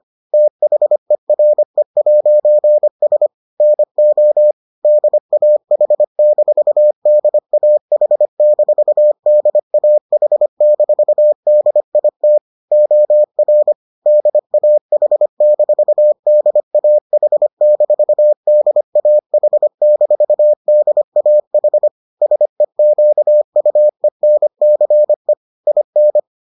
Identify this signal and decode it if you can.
Morse code, heard as a keyed tone, about 25 wpm, 600 Hz. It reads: THERE'S NO DAH-DAH-DAH-DIT OR DAH-DAH-DAH-DAH SEQUENCE IN